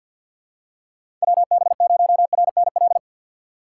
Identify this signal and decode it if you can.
Morse code, heard as a keyed tone, about 49 wpm, 700 Hz. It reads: WB0RNL
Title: USA Call Signs - 6 Characters - 3x Character Spacing - 50wpm